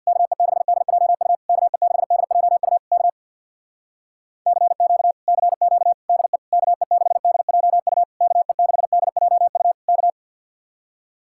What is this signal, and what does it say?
Morse code, heard as a keyed tone, about 64 wpm, 700 Hz. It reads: KE6DJU KE6DJU K CQ CQ DE KE6DJU KE6DJU K